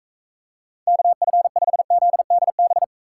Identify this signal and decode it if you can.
Morse code, heard as a keyed tone, about 39 wpm, 700 Hz. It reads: KF5ZDB